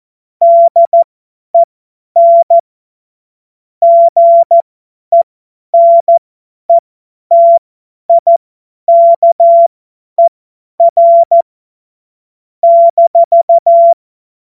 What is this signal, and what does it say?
Morse code, heard as a keyed tone, about 14 wpm, spaced as for 10 wpm, 680 Hz. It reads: DEN GENETIKER -
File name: Morsekurs / Witze / Witz 06. Januar 2026 50 F70